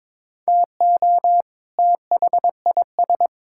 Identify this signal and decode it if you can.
Morse code, heard as a keyed tone, about 22 wpm, 700 Hz. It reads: TO THIS
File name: Build A Phrase - 2-Word Phrases - 22wpm